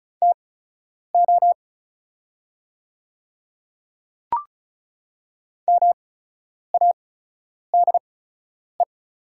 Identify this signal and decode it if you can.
Morse code, heard as a keyed tone, about 34 wpm, spaced as for 10 wpm, 690 Hz. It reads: TO MADE
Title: 8x Character Spacing - Top 200 Words - 35wpm